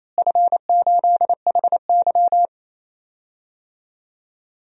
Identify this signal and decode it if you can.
Morse code, heard as a keyed tone, about 28 wpm, 700 Hz. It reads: F8HY
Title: DXCC-100 Call Signs - 4 Characters - Repeated 3x - 28wpm